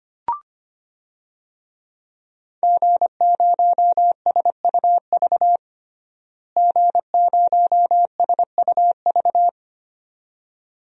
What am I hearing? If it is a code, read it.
Morse code, heard as a keyed tone, about 25 wpm, 700 Hz. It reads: G0SUV G0SUV
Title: DXCC-100 Call Signs - 5 Characters - Repeated 2x - 25wpm